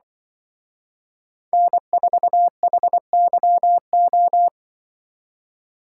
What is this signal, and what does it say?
Morse code, heard as a keyed tone, about 23 wpm, 700 Hz. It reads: N4HYO
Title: Speed Racing - USA 1x3 Call Signs 20wpm